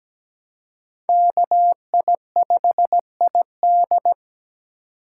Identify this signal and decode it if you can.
Morse code, heard as a keyed tone, about 20 wpm, 700 Hz. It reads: KI5ID
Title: Speed Racing - USA 2x2 Call Signs 17wpm